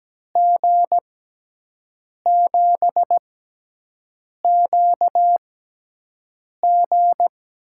Morse code, heard as a keyed tone, about 17 wpm, spaced as for 6 wpm, 700 Hz. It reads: G7QG
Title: USA Call Signs - 6 Characters - 6x Character Spacing - 17wpm